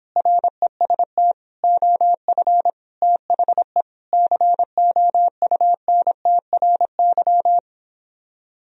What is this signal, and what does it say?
Morse code, heard as a keyed tone, about 26 wpm, 700 Hz. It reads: REST OF THE COUNTRY